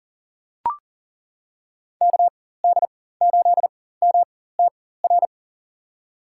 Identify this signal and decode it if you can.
Morse code, heard as a keyed tone, about 40 wpm, spaced as for 19 wpm, 700 Hz. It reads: KD8MTR
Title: USA Call Signs - 6 Characters - 4x Character Spacing - 40wpm